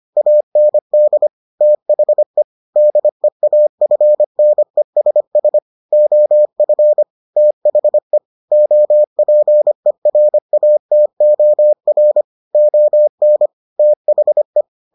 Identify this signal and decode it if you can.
Morse code, heard as a keyed tone, about 25 wpm, 600 Hz. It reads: AND THE DEAFNESS OF THE OPERATOR ON THE